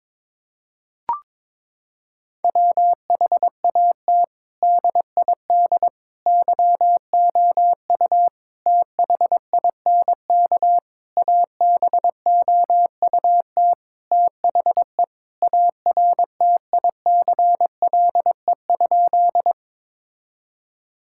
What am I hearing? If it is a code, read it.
Morse code, heard as a keyed tone, about 22 wpm, 700 Hz. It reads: WHAT DID YOU THINK ABOUT THE ARTICLE?
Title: Sentences from Top 1000 Words - Review Part 5 - 22wpm